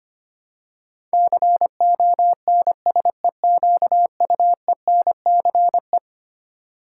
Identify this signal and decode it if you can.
Morse code, heard as a keyed tone, about 25 wpm, 700 Hz. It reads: CONSEQUENCE
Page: morse-code-ninja-practice-25wpm